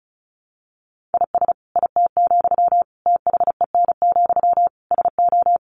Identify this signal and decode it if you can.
Morse code, heard as a keyed tone, about 34 wpm, 700 Hz. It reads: IS IT, THEN, SO
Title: Sentences from Top 1500 Words 35wpm